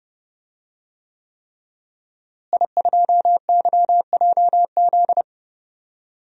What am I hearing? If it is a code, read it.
Morse code, heard as a keyed tone, about 29 wpm, 700 Hz. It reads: I2YJZ
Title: DXCC-100 Call Signs - 5 Characters - Repeated 3x - 30wpm